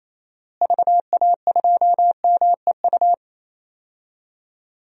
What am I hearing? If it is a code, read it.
Morse code, heard as a keyed tone, about 27 wpm, 700 Hz. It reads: VA2MEU
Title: DXCC-100 Call Signs - 6 Characters - Repeated 3x - 28wpm